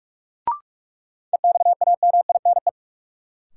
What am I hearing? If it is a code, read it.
Morse code, heard as a keyed tone, about 45 wpm, 700 Hz. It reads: EXAMINE
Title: Sentences from Top 1600 Words - New Word Drill Alt - 45wpm